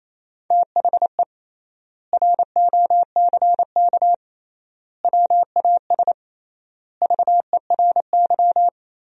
Morse code, heard as a keyed tone, about 28 wpm, 700 Hz. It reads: THE ROCK WAS VERY
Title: 3x Word-Spacing - Sentences from Top 400 Words - 28wpm